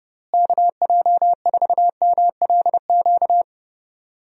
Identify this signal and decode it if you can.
Morse code, heard as a keyed tone, about 30 wpm, 700 Hz. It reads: KJ4MLQ
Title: USA Call Signs - 6 Characters - Repeated 3x - 30wpm